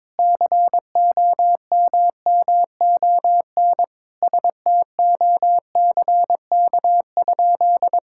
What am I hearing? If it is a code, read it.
Morse code, heard as a keyed tone, about 22 wpm, 700 Hz. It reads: COMMON STOCK?
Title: Sentences from Top 1000 Words - Review Part 5 - 22wpm